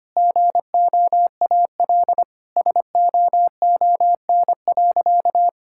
Morse code, heard as a keyed tone, about 25 wpm, 700 Hz. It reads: GOAL SOON.
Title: Sentences from Top 1000 Words - Review Part 6 - 25wpm